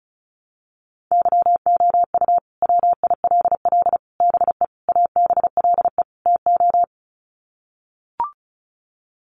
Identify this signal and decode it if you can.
Morse code, heard as a keyed tone, about 34 wpm, 700 Hz. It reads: YOU WILL BE ABLE TO
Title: Sets of 5 Words Encore2 35wpm